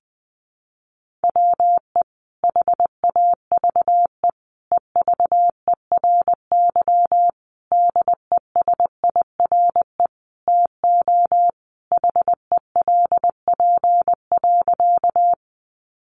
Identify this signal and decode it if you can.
Morse code, heard as a keyed tone, about 20 wpm, 700 Hz. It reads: WE HAVE EVERY DESIRE TO HELP.
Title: Sentences from Top 1100 Words 20wpm